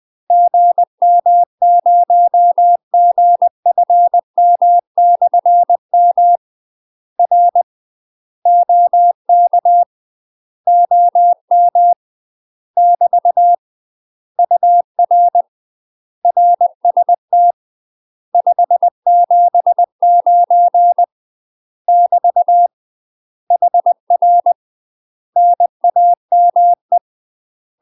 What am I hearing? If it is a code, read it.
Morse code, heard as a keyed tone, about 20 wpm, 700 Hz. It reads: GM0GFM/M R OK OM <BT> UR RST 579 <BT> HR NAME
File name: UK MASTER QSOs TRAINING / 20WPM / UK MASTER QSOs 0047